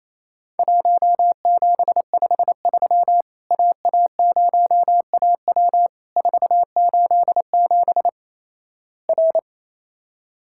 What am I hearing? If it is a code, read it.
Morse code, heard as a keyed tone, about 28 wpm, 700 Hz. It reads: 1753 AA0AW 487 R
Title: Worked All Europe DX Contest - QTC Exchange - 28wpm